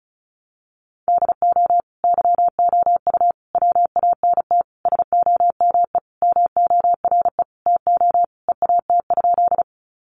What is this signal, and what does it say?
Morse code, heard as a keyed tone, about 34 wpm, 700 Hz. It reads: DO YOU WANT SOME MORE TO EAT?
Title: Sentences from Top 300 Words 35wpm